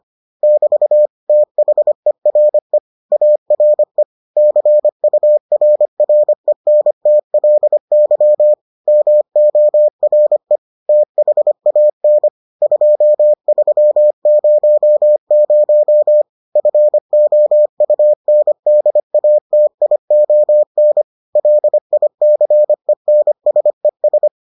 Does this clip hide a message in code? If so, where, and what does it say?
Yes: Morse code from 0.4 to 24.3 s, heard as a keyed tone, about 25 wpm, 600 Hz. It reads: <BT> THERE ARE CURRENTLY MORE THAN 2300 FOUNDATION LICENSES